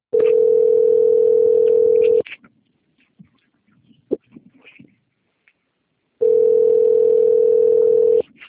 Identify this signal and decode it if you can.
Ringback tone (North America), 440 + 480 Hz.